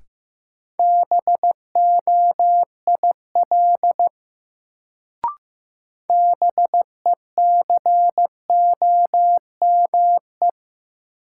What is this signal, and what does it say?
Morse code, heard as a keyed tone, about 15 wpm, 700 Hz. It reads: BOIL BECOME
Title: Sets of 5 Words Encore - New Word Drill - 15wpm